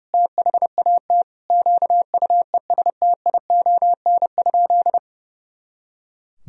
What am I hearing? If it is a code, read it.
Morse code, heard as a keyed tone, about 30 wpm, 700 Hz. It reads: THAT QUESTION?
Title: Sentences from Top 400 Words and Abbreviations 30wpm